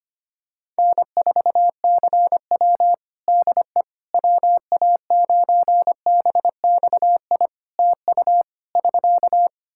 Morse code, heard as a keyed tone, about 25 wpm, 700 Hz. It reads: N4CW DE WA9BXI TU <SK>
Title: Practice QSO - Exchange 4 - Ending - 25wpm